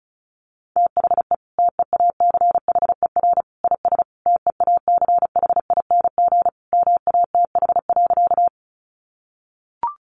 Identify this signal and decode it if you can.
Morse code, heard as a keyed tone, about 34 wpm, 700 Hz. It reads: THE TEACHER IS TEACHING MATH.
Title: Easy Phrases 35wpm